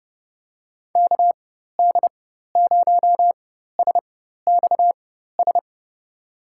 Morse code, heard as a keyed tone, about 30 wpm, spaced as for 14 wpm, 700 Hz. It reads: KD0SXS
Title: USA Call Signs - 6 Characters - 4x Character Spacing - 30wpm